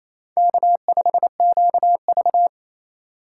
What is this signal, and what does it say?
Morse code, heard as a keyed tone, about 28 wpm, 700 Hz. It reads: K5QV
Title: USA Call Signs - 4x Character Spacing - 28wpm